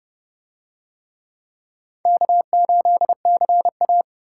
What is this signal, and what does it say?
Morse code, heard as a keyed tone, about 29 wpm, 700 Hz. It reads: K8CA